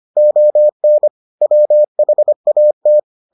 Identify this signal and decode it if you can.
Morse code, heard as a keyed tone, about 25 wpm, 600 Hz. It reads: ON WHAT